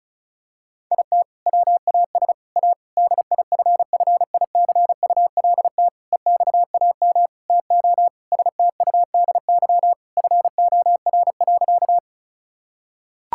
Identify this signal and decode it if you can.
Morse code, heard as a keyed tone, about 36 wpm, 690 Hz. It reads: IT WAS A DIFFICULT EXAM TO STUDY FOR.
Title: Sentences from Top 1000 Words - Review Part 5 - 35wpm